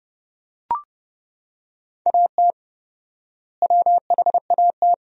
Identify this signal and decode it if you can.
Morse code, heard as a keyed tone, about 32 wpm, 700 Hz. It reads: AT WHAT
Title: 4x Word-Spacing - Sentences from Top 100 Words - 30wpm